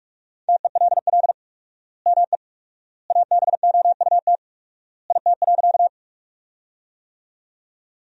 Morse code, heard as a keyed tone, about 41 wpm, 700 Hz. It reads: TELL ME ABOUT IT.
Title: 4x Word-Spacing - Sentences from Top 200 Words - 45wpm